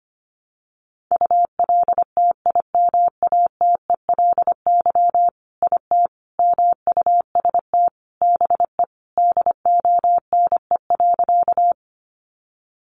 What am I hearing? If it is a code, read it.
Morse code, heard as a keyed tone, about 25 wpm, 700 Hz. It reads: ULTIMATELY IT MUST BE DONE.
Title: Sentences from Top 1200 Words 25wpm